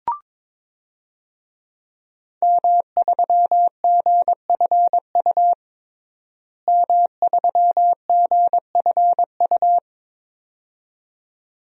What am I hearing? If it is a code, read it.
Morse code, heard as a keyed tone, about 22 wpm, 700 Hz. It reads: M3GFU M3GFU
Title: DXCC-100 Call Signs - 5 Characters - Repeated 2x - 22wpm